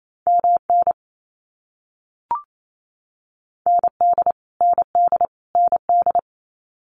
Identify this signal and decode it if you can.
Morse code, heard as a keyed tone, about 28 wpm, 700 Hz. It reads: MN ND ND ND
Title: US States Repeated 3x 28wpm